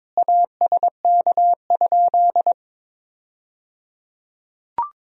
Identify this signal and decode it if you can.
Morse code, heard as a keyed tone, about 22 wpm, 700 Hz. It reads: ASK?
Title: 3x Word-Spacing - Sentences from Top 200 Words - 22wpm